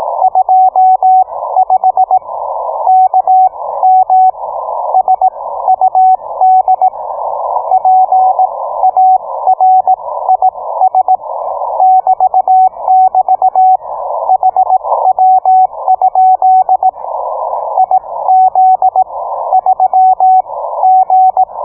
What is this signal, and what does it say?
Morse code, heard as a keyed tone, about 18 wpm, spaced as for 14 wpm, 750 Hz, with radio noise. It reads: 25 KM SUD PARIS <BT><BT> HW? IZ3G